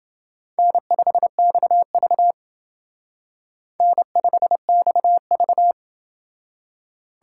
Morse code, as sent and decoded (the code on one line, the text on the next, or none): -. ..... -..- ...- / -. ..... -..- ...-
N5XV N5XV